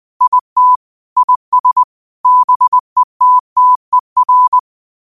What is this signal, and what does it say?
Morse code, heard as a keyed tone, about 20 wpm, 1000 Hz. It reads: IT IS BETTER
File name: qotdfiles / 20wpm